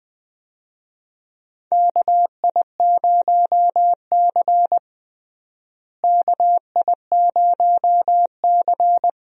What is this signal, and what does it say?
Morse code, heard as a keyed tone, about 21 wpm, 700 Hz. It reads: KI0C KI0C
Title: USA Call Signs - 4 Characters - Repeated 2x - 20wpm